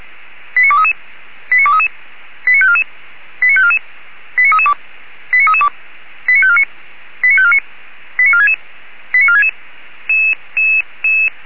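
Five-tone selective call (ZVEI), heard as quick runs of five tones, tones of about 70 ms. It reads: 89233 89233 89544 89544 89332 89332 89549 89549 89466 89466
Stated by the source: Five-tone ZVEI selective call 89386, received by SDR.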